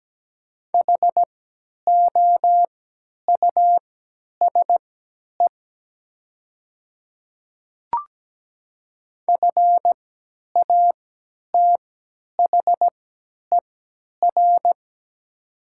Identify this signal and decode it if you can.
Morse code, heard as a keyed tone, about 17 wpm, spaced as for 10 wpm, 700 Hz. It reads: HOUSE FATHER
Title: 3x Character Spacing - Top 500 Words - 17wpm